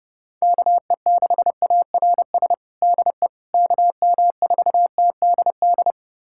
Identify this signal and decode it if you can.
Morse code, heard as a keyed tone, about 32 wpm, 700 Hz. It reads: KE6ARS DE KM4TDD
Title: USA Callsign DE Callsign - 6 Characters - 30wpm